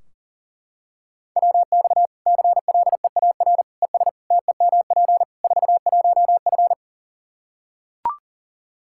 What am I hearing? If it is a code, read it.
Morse code, heard as a keyed tone, about 40 wpm, 700 Hz. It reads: WX CLEAR ES TEMP 41F
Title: K1USN SST QSO Practice 40wpm